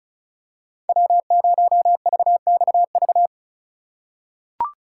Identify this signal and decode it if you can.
Morse code, heard as a keyed tone, about 36 wpm, 700 Hz. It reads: W0VXV
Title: USA Call Signs - 5 Characters - 2x Character Spacing - 35wpm